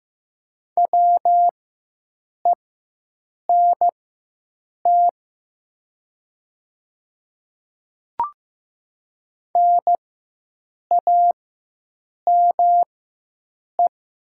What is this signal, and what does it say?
Morse code, heard as a keyed tone, about 15 wpm, spaced as for 7 wpm, 700 Hz. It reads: WENT NAME
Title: 4x Character Spacing - Top 300 Words - 15wpm